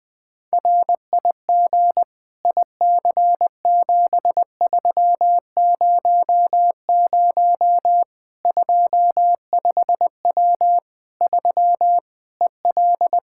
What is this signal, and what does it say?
Morse code, heard as a keyed tone, about 20 wpm, 700 Hz. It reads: RIG IC7300 25W 3 EL
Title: Practice QSO - Exchange 2 - Info - 20wpm